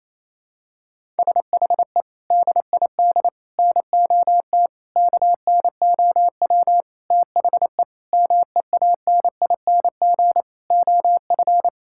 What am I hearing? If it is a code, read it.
Morse code, heard as a keyed tone, about 28 wpm, 700 Hz. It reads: SHE DID NOT KNOW THE MEANING OF
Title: Sentences from Top 1000 Words - Review Part 6 - 28wpm